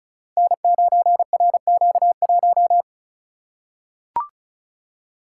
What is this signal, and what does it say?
Morse code, heard as a keyed tone, about 36 wpm, 690 Hz. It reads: N9RQ1
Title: Code Groups - 5 Letter:Numbers - 35wpm